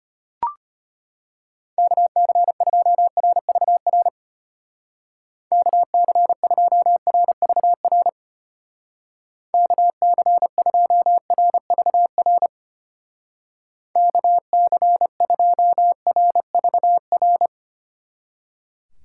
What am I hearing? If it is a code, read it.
Morse code, heard as a keyed tone, about 30 wpm, 700 Hz. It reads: KC2RVR KC2RVR KC2RVR KC2RVR
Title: Speed Racing - USA Call Signs 25wpm